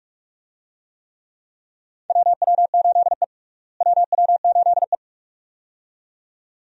Morse code, heard as a keyed tone, about 47 wpm, 700 Hz. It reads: WW8E WW8E
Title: USA Call Signs - 4 Characters - Repeated 2x - 45wpm